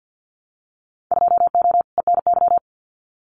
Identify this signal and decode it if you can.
Morse code, heard as a keyed tone, about 54 wpm, 700 Hz. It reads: 2O ENY
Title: Winter Field Day Exchanges Encore 50wpm